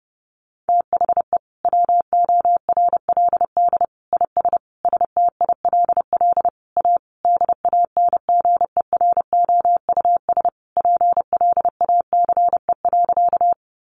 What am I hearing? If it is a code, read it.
Morse code, heard as a keyed tone, about 28 wpm, 700 Hz. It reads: THE WORLD IS STILL A DANGEROUS PLACE.
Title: Sentences from Top 1200 Words 30wpm